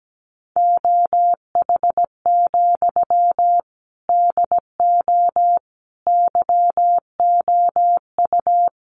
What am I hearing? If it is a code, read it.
Morse code, heard as a keyed tone, about 17 wpm, 700 Hz. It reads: OH, DO YOU
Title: Sentences from Top 500 Words 17wpm